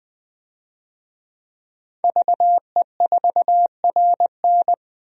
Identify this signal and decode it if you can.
Morse code, heard as a keyed tone, about 21 wpm, 700 Hz. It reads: VE4RN
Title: DXCC-100 Call Signs - 5 Characters - Repeated 3x - 20wpm